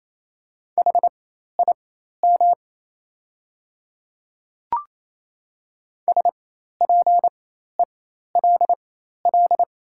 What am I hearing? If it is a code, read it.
Morse code, heard as a keyed tone, about 28 wpm, spaced as for 13 wpm, 690 Hz. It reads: HIM SPELL
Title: 4x Character Spacing - Top 200 Words - 28wpm